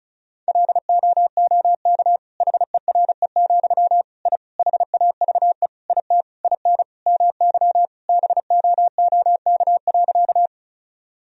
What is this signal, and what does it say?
Morse code, heard as a keyed tone, about 36 wpm, 700 Hz. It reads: LOOK HERE, I HAVE IT IN MY BOOK.
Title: Sentences from Top 1000 Words - Review Part 1 - 35wpm